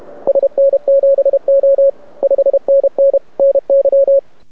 Morse code, heard as a keyed tone, about 30 wpm, 550 Hz, with radio noise. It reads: SN7O 5NN NY
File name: rec contest / arrl cw 2018 / test 32